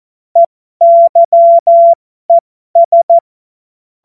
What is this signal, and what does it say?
Morse code, heard as a keyed tone, about 14 wpm, spaced as for 12 wpm, 680 Hz. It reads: EYES
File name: Morsekurs / Witze